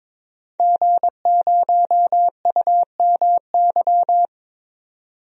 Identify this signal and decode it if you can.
Morse code, heard as a keyed tone, about 22 wpm, 700 Hz. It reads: G0UMY
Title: DXCC-100 Call Signs - 5 Characters - 22wpm